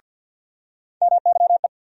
Morse code, heard as a keyed tone, about 49 wpm, 700 Hz. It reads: MYE
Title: USA Call Sign Postfixes - 3 Letters 50wpm